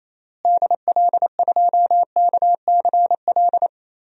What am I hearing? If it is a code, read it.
Morse code, heard as a keyed tone, about 28 wpm, 700 Hz. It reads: DL2KCL
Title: DXCC-100 Call Signs - 6 Characters - Repeated 3x - 28wpm